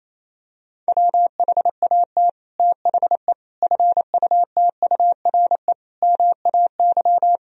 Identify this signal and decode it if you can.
Morse code, heard as a keyed tone, about 28 wpm, 700 Hz. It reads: WHAT THE FUTURE MAY
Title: Sentences from Top 1000 Words - Review Part 2 - 28wpm